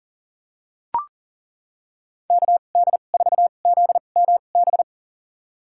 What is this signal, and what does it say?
Morse code, heard as a keyed tone, about 40 wpm, spaced as for 29 wpm, 700 Hz. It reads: KD4ZMB
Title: USA Call Signs - 6 Characters - 2x Character Spacing - 40wpm